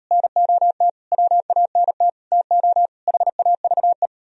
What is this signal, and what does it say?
Morse code, heard as a keyed tone, about 38 wpm, 700 Hz. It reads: NOT WANT TO HAVE